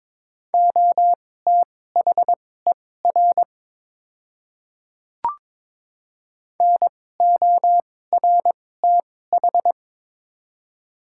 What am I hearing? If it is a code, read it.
Morse code, heard as a keyed tone, about 22 wpm, spaced as for 16 wpm, 700 Hz. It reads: OTHER NORTH
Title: 2x Character Spacing - Top 300 Words - 22wpm